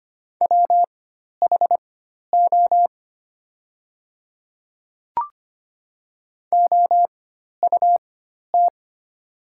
Morse code, heard as a keyed tone, about 25 wpm, spaced as for 12 wpm, 700 Hz. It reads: WHO OUT
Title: 4x Character Spacing - Top 100 Words - 25wpm